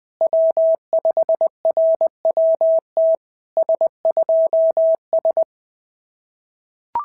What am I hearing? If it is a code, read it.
Morse code, heard as a keyed tone, about 19 wpm, 650 Hz. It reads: W5RWT S2S
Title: SOTA QSO Practice 20wpm